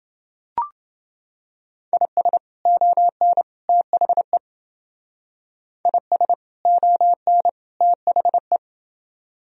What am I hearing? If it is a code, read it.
Morse code, heard as a keyed tone, about 30 wpm, 700 Hz. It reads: IS ON THE IS ON THE